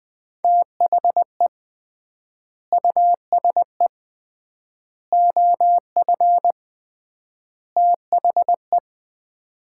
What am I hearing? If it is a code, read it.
Morse code, heard as a keyed tone, about 20 wpm, 700 Hz. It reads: THE USE OF THE
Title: 3x Word-Spacing - Sets of 4 Words - 20wpm